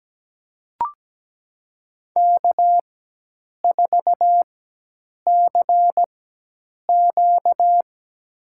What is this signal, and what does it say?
Morse code, heard as a keyed tone, about 17 wpm, spaced as for 8 wpm, 700 Hz. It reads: K4CQ
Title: USA Call Signs - 4 Characters - 4x Character Spacing - 17wpm